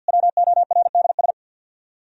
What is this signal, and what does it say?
Morse code, heard as a keyed tone, about 60 wpm, 700 Hz. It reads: WORDS
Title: Speed-Racing - Sets of 3 Words 50wpm